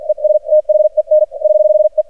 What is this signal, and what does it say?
Morse code, heard as a keyed tone, about 48 wpm, spaced as for 41 wpm, 610 Hz. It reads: ES THEN <HH> E
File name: archived / CRSnet